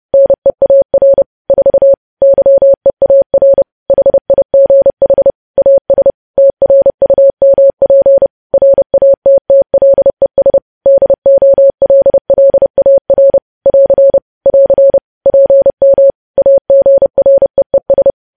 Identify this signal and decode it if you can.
Morse code, heard as a keyed tone, about 30 wpm, 550 Hz. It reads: NEAR 4 YEAR HIGH AS TRUMP RATTLES DOLLAR <AR> <AR> PM AGREES
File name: podcast / News headlines-1x-30x30